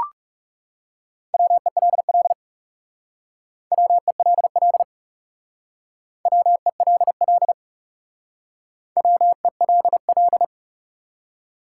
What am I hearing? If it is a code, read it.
Morse code, heard as a keyed tone, about 40 wpm, 700 Hz. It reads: WELL WELL WELL WELL